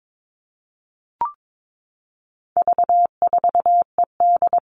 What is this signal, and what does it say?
Morse code, heard as a keyed tone, about 22 wpm, 700 Hz. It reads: V4ED
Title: All Cut Numbers - 4-Digit 22wpm